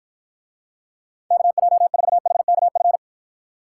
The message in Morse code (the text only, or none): KJ4HKF